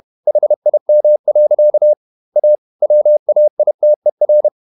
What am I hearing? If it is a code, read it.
Morse code, heard as a keyed tone, about 31 wpm, 600 Hz. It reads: HIM. A WAITER
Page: Never 31wpm | CW med Gnister